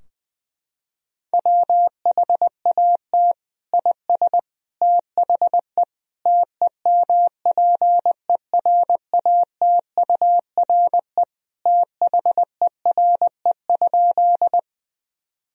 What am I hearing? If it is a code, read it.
Morse code, heard as a keyed tone, about 21 wpm, 700 Hz. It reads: WHAT IS THE TEMPERATURE THERE?
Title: Sentences from Top 1000 Words - Review Part 4 - 20wpm